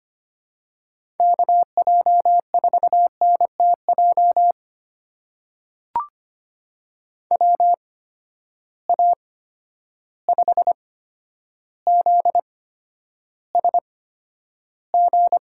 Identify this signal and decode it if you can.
Morse code, heard as a keyed tone, about 25 wpm, spaced as for 11 wpm, 700 Hz. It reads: KJ4NTJ WA5ZSG
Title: USA Call Signs - 6 Characters - 8x Character Spacing - 25wpm